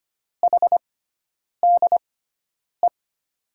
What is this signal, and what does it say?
Morse code, heard as a keyed tone, about 25 wpm, spaced as for 9 wpm, 700 Hz. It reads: HDE
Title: USA Call Signs - 6 Characters - 6x Character Spacing - 25wpm